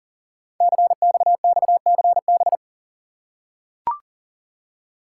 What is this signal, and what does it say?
Morse code, heard as a keyed tone, about 39 wpm, 700 Hz. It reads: CXXCB
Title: Code Groups - 5 Letters - 40wpm